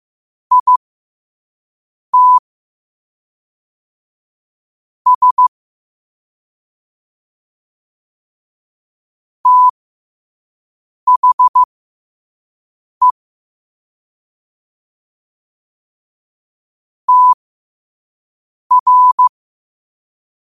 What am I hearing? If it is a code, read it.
Morse code, heard as a keyed tone, about 15 wpm, spaced as for 5 wpm, 1000 Hz. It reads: IT S THE TR
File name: qotdfiles / 5wpm